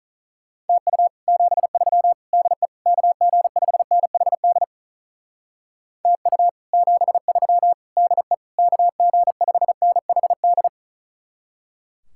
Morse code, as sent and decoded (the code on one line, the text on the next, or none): - ..- / --... ...-- / -.. / -.- --. ..... -. .... -.. / - ..- / --... ...-- / -.. / -.- --. ..... -. .... -..
TU 73 DE KG5NHD TU 73 DE KG5NHD